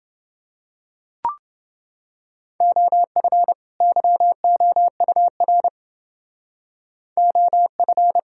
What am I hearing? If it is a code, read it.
Morse code, heard as a keyed tone, about 30 wpm, 700 Hz. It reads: OF YOUR OF